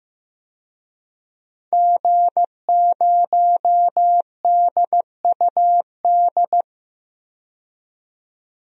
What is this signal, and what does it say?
Morse code, heard as a keyed tone, about 15 wpm, 700 Hz. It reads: G0DUD
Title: DXCC-100 Call Signs - 5 Characters - 15wpm